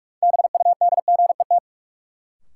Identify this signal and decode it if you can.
Morse code, heard as a keyed tone, about 46 wpm, 700 Hz. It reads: BUDGET
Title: Sentences from Top 1200 Words - New Word Drill Alt - 45wpm